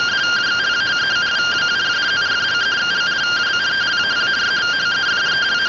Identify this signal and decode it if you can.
Two-tone teleprinter signal, about 170 Hz shift.